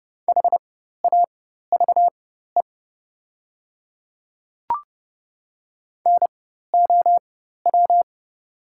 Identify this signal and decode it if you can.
Morse code, heard as a keyed tone, about 30 wpm, spaced as for 14 wpm, 700 Hz. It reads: HAVE NOW
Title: 4x Character Spacing - Top 100 Words - 30wpm